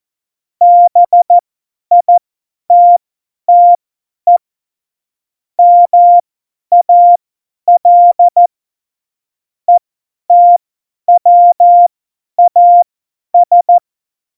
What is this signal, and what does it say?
Morse code, heard as a keyed tone, about 14 wpm, spaced as for 10 wpm, 700 Hz. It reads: BITTE MAL ETWAS
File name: Morsekurs / Witze / Witz 16. April 2019, 50 F70